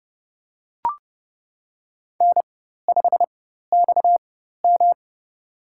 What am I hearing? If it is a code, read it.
Morse code, heard as a keyed tone, about 31 wpm, spaced as for 14 wpm, 700 Hz. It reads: N5XM
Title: USA Call Signs - 4 Characters - 4x Character Spacing - 30wpm